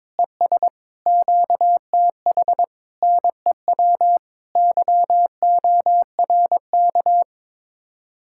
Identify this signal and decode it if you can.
Morse code, heard as a keyed tone, about 22 wpm, 700 Hz. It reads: ES QTH NEW YORK
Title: Speed-Racing Common QSO Phrases - 22wpm